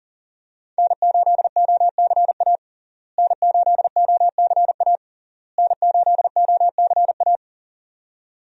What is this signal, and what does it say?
Morse code, heard as a keyed tone, about 40 wpm, 700 Hz. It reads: N8OCA N8OCA N8OCA
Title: USA Call Signs - 5 Characters - Repeated 3x - 40wpm